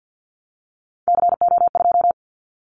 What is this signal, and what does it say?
Morse code, heard as a keyed tone, about 50 wpm, 700 Hz. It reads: CO2
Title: Callsign Prefixes for Top 100 DXCC - 50wpm